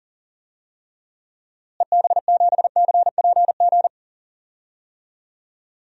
Morse code, heard as a keyed tone, about 40 wpm, 700 Hz. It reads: EB7CPG
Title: DXCC-100 Call Signs - 6 Characters - Repeated 2x - 40wpm